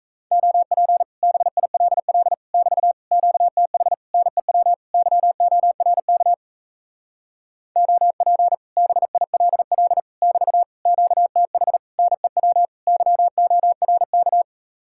Morse code, heard as a keyed tone, about 38 wpm, 700 Hz. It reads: OP BILL <BT> QTH NEW YORK OP BILL <BT> QTH NEW YORK